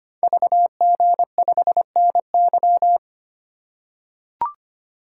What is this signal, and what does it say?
Morse code, heard as a keyed tone, about 25 wpm, 700 Hz. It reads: VG5NY
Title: Code Groups - 5 Letter:Numbers - 25wpm